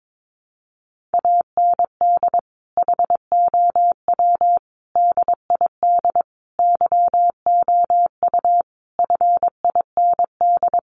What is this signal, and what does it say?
Morse code, heard as a keyed tone, about 22 wpm, 700 Hz. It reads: AND HOW DID YOU FIND
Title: Sentences from Top 100 Words 22wpm